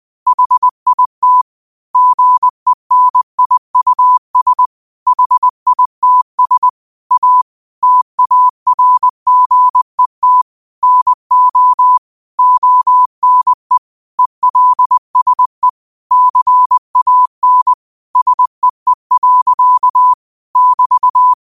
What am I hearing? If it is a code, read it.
Morse code, heard as a keyed tone, about 20 wpm, 1000 Hz. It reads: HIT GENIUS HITS A TARGET NO ONE ELSE CAN SEE. <BT>